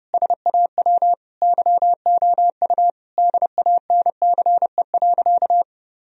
Morse code, heard as a keyed tone, about 30 wpm, 700 Hz. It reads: SAW YOU DANCE.
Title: Sentences from Top 1000 Words - Review Part 4 - 30wpm